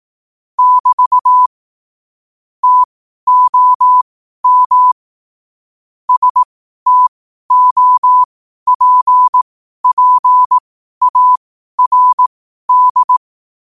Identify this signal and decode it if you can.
Morse code, heard as a keyed tone, about 18 wpm, spaced as for 12 wpm, 1000 Hz. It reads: <BT> TOM STOPPARD
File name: qotdfiles / 12wpm